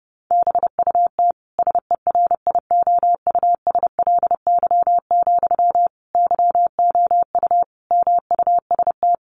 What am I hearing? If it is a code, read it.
Morse code, heard as a keyed tone, about 28 wpm, 700 Hz. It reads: BUT SERIOUSLY, YOU MUST
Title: Sentences from Top 1200 Words 30wpm